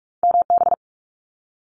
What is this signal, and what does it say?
Morse code, heard as a keyed tone, about 46 wpm, 700 Hz. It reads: MB
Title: Multi-letter Phonemes Quick Fire 45wpm